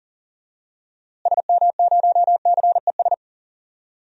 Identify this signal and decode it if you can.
Morse code, heard as a keyed tone, about 40 wpm, 700 Hz. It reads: SM0CES
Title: DXCC-100 Call Signs - 6 Characters - Repeated 2x - 40wpm